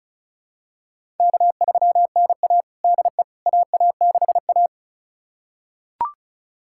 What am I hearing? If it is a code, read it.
Morse code, heard as a keyed tone, about 34 wpm, 700 Hz. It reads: K3NA DE AA6A
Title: USA Callsign DE Callsign - 4 Characters - 35wpm